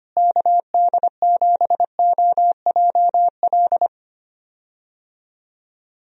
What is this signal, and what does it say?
Morse code, heard as a keyed tone, about 25 wpm, 700 Hz. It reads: KD7OJL
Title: USA Call Signs - 6 Characters - Repeated 2x - 25wpm